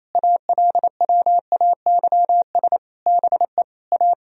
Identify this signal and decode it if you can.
Morse code, heard as a keyed tone, about 28 wpm, 700 Hz. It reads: ALWAYS BE A
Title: Sentences from Top 1000 Words - Review Part 4 - 28wpm